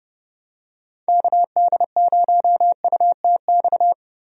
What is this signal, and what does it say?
Morse code, heard as a keyed tone, about 30 wpm, 700 Hz. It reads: KD0UTX
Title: USA Call Signs - 6 Characters - Repeated 2x - 30wpm